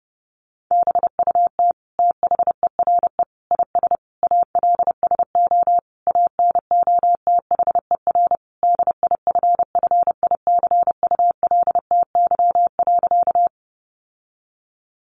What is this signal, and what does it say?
Morse code, heard as a keyed tone, about 30 wpm, 700 Hz. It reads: BUT THERE IS ALSO ANOTHER DIFFICULTY.
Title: Sentences from Top 1500 Words 30wpm